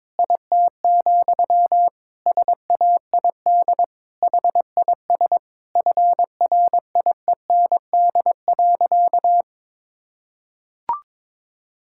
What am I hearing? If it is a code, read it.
Morse code, heard as a keyed tone, about 22 wpm, 700 Hz. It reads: IT, SAID HIS FRIEND.
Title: Sentences from Top 1000 Words - Review Part 5 - 22wpm